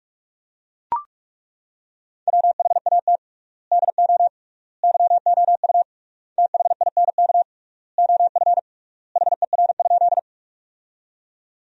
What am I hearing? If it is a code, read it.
Morse code, heard as a keyed tone, about 42 wpm, 700 Hz. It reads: WHAT DO YOU THINK OF HER?
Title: 3x Word-Spacing - Sentences from Top 200 Words - 45wpm